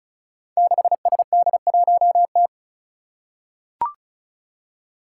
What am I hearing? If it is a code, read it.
Morse code, heard as a keyed tone, about 36 wpm, 700 Hz. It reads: 6SD1T
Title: Five Challenging Letter-Number Combinations 35wpm